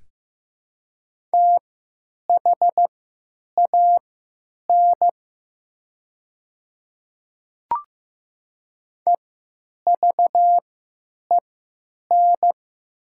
Morse code, heard as a keyed tone, about 15 wpm, spaced as for 9 wpm, 700 Hz. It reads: THAN EVEN
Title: 3x Character Spacing - Top 200 Words - 15wpm